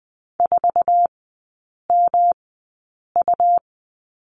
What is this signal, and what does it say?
Morse code, heard as a keyed tone, about 20 wpm, spaced as for 8 wpm, 700 Hz. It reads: 4MU
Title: 178 Lesson 29e - Slash - Three Characters 20wpm